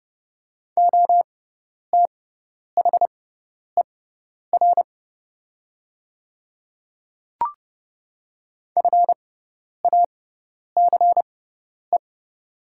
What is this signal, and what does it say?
Morse code, heard as a keyed tone, about 30 wpm, spaced as for 10 wpm, 700 Hz. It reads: OTHER FACE
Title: 6x Character Spacing - Top 300 Words - 30wpm